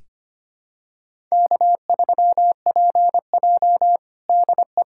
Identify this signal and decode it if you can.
Morse code, heard as a keyed tone, about 25 wpm, 700 Hz. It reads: K3PJ DE
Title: USA Callsign DE Callsign - 4 Characters - 25wpm